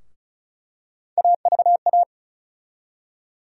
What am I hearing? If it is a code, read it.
Morse code, heard as a keyed tone, about 36 wpm, 700 Hz. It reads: AVA
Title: USA Call Sign Postfixes - 3 Letters 35wpm